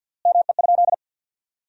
Morse code, heard as a keyed tone, about 50 wpm, 700 Hz. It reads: ME?
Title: 3x Word-Spacing - Sentences from Top 400 Words - 50wpm